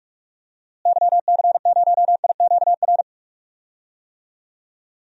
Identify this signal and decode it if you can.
Morse code, heard as a keyed tone, about 46 wpm, 700 Hz. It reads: YC0IQR
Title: DXCC-100 Call Signs - 6 Characters - Repeated 2x - 45wpm